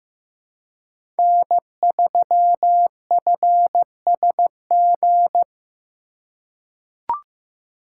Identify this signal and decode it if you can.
Morse code, heard as a keyed tone, about 15 wpm, 700 Hz. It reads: N3FSG
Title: USA Call Signs - 5 Characters - 3x Character Spacing - 15wpm